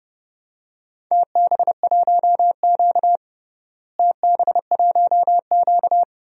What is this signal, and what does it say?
Morse code, heard as a keyed tone, about 31 wpm, 700 Hz. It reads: TB1Q TB1Q
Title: DXCC-100 Call Signs - 4 Characters - Repeated 3x - 30wpm